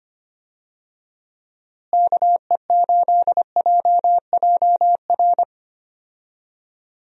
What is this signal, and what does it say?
Morse code, heard as a keyed tone, about 25 wpm, 700 Hz. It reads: KE8JJR
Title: USA Call Signs - 6 Characters - Repeated 3x - 25wpm